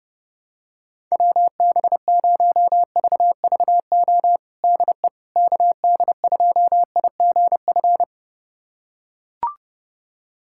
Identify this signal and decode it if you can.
Morse code, heard as a keyed tone, about 30 wpm, 700 Hz. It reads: WB0VVO DE KD2IGF
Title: USA Callsign DE Callsign - 6 Characters - 30wpm